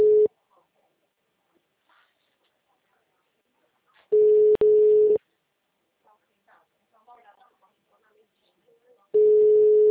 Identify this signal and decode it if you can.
Ringback tone (Europe, 425 Hz), with speech.
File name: out- -2009- - - .7121